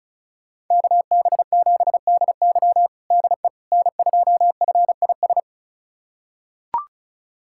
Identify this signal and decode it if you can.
Morse code, heard as a keyed tone, about 34 wpm, 700 Hz. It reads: KB7DY DE N2FIS
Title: USA Callsign DE Callsign - 5 Characters - 35wpm